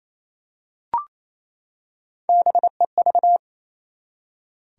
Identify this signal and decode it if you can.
Morse code, heard as a keyed tone, about 28 wpm, 700 Hz. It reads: BEV
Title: USA Common Names - Max 3 Letters - 28wpm